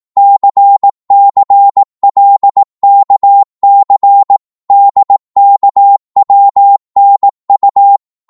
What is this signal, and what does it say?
Morse code, heard as a keyed tone, about 18 wpm, 800 Hz. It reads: CCLKC DKWNU
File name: Morse / ltr-18wpm